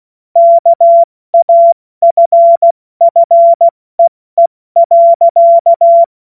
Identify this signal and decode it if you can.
Morse code, heard as a keyed tone, about 16 wpm, spaced as for 14 wpm, 680 Hz. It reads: KAFFEE.
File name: Morsekurs / Witze / Witz 25. Mai 2021, 70 F80